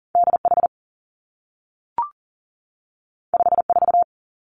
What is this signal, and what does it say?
Morse code, heard as a keyed tone, about 39 wpm, 700 Hz. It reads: DH 54
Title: Two Challenging Letter-Number Combinations 40wpm